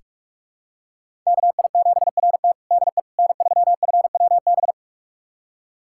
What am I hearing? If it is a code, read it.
Morse code, heard as a keyed tone, about 45 wpm, 700 Hz. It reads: KI7RT DE N3FWB
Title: USA Callsign DE Callsign - 5 Characters - 45wpm